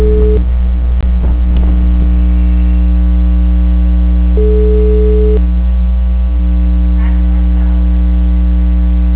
Ringback tone (Europe, 425 Hz).
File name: out- -2061- - - .72989